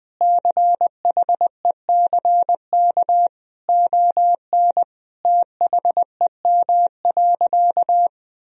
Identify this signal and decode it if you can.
Morse code, heard as a keyed tone, about 20 wpm, 700 Hz. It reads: CHECK ON THEM.
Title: Sentences from Top 1000 Words - Review Part 3 - 20wpm